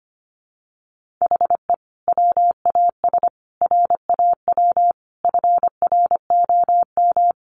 Morse code, heard as a keyed tone, about 25 wpm, 700 Hz. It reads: HE WAS RAW FROM
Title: Sentences from Top 1500 Words 25wpm